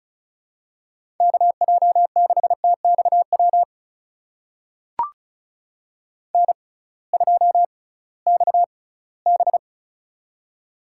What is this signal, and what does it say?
Morse code, heard as a keyed tone, about 36 wpm, spaced as for 20 wpm, 700 Hz. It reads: KJ6TXW N2XB
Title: USA Call Signs - 6x Character Spacing - 35wpm